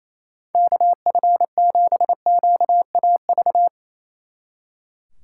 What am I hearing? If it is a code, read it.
Morse code, heard as a keyed tone, about 34 wpm, 700 Hz. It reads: KF7QAV
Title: Speed Racing - USA Call Signs 28wpm